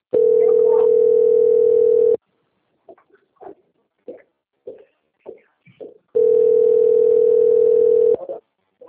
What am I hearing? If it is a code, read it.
Ringback tone (North America), 440 + 480 Hz.